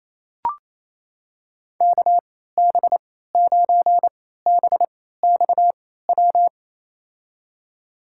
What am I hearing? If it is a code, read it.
Morse code, heard as a keyed tone, about 28 wpm, spaced as for 16 wpm, 700 Hz. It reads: KB9BXW
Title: USA Call Signs - 6 Characters - 3x Character Spacing - 28wpm